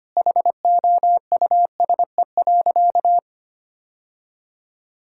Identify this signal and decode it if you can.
Morse code, heard as a keyed tone, about 25 wpm, 700 Hz. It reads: HOUSE.
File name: practice / 25wpm / Sentences with 2-Word Phrases - 25wpm